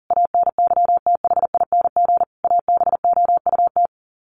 Morse code, heard as a keyed tone, about 40 wpm, 700 Hz. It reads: ANYTHING ABOUT
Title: Sets of 5 Words 40wpm